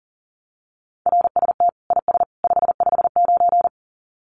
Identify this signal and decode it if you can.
Morse code, heard as a keyed tone, about 40 wpm, 700 Hz. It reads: RST IS 559
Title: Common QSO Phrases 40wpm